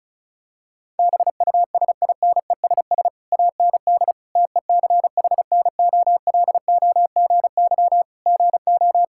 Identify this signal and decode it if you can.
Morse code, heard as a keyed tone, about 36 wpm, 700 Hz. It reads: BUSINESS AND TECHNOLOGY GO
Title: Sentences from Top 1000 Words - Review Part 5 - 35wpm